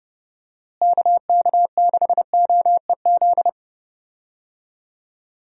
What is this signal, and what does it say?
Morse code, heard as a keyed tone, about 30 wpm, 700 Hz. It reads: KK6OEZ
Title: USA Call Signs - 6 Characters - Repeated 3x - 30wpm